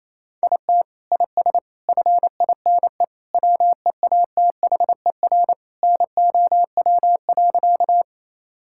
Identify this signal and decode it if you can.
Morse code, heard as a keyed tone, about 28 wpm, 700 Hz. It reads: IT IS FINE WEATHER NOW.
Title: Sentences from Top 1000 Words - Review Part 5 - 28wpm